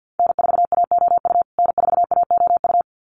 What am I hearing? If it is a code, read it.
Morse code, heard as a keyed tone, about 50 wpm, 700 Hz. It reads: N4AOU N4AOU
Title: USA Call Signs Repeated 2x 50wpm